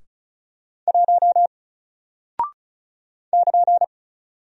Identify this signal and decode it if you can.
Morse code, heard as a keyed tone, about 36 wpm, 700 Hz. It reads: 1 <KN>
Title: Single Character (Letter:Number:Punctuation:Prosign) 35wpm